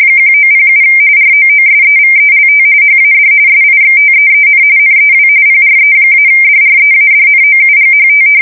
RTTY, about 45.45 baud. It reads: 1907 ROOSEVELT HAD SIXTEEN NEW PRE-DREADNOUGHT
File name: fsk170 / usnavy3